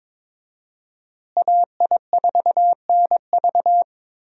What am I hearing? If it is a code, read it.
Morse code, heard as a keyed tone, about 22 wpm, 700 Hz. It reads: AI4NV
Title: USA Call Signs - 5 Characters - Repeated 2x - 22wpm